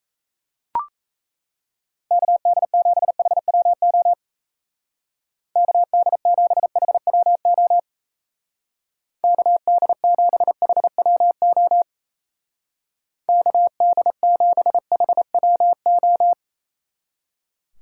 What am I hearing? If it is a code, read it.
Morse code, heard as a keyed tone, about 34 wpm, 700 Hz. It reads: KD7HWO KD7HWO KD7HWO KD7HWO
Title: Speed Racing - USA 2x3 Call Signs 28wpm